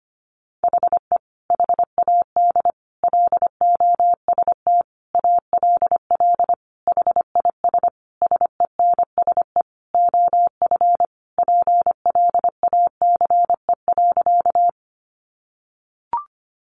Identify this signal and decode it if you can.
Morse code, heard as a keyed tone, about 25 wpm, 700 Hz. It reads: HE HAD LOST ALL HIS SENSE OF PLACE.
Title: Sentences from Top 1400 Words 25wpm